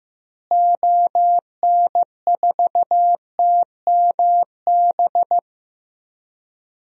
Morse code, heard as a keyed tone, about 15 wpm, 700 Hz. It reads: ON4TMB
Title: DXCC-100 Call Signs - 6 Characters - Repeated 2x - 15wpm